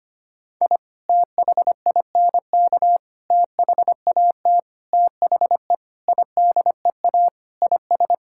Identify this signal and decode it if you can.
Morse code, heard as a keyed tone, about 25 wpm, 700 Hz. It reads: I THINK THAT THE IDEA IS
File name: practice / 25wpm / Sentences with 4-Word Phrases - 25wpm